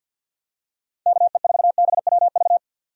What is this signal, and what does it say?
Morse code, heard as a keyed tone, about 51 wpm, 700 Hz. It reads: KE4BWV
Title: USA Call Signs - 6 Characters - 50wpm